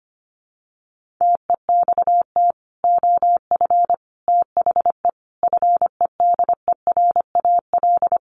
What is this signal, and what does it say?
Morse code, heard as a keyed tone, about 25 wpm, 700 Hz. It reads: TEXT OF THE FEDERAL
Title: Sentences from Top 1400 Words 25wpm